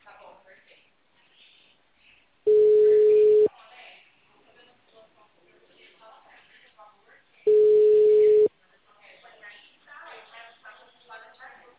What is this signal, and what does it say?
Ringback tone (Europe, 425 Hz), with speech.